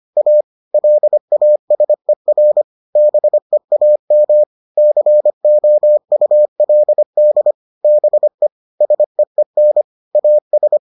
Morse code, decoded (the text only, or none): A LASER BEAM COULD BE SEEN AS